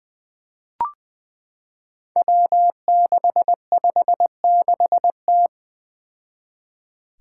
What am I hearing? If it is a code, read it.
Morse code, heard as a keyed tone, about 20 wpm, 700 Hz. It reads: W656T
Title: Five Challenging Letter-Number Combinations 20wpm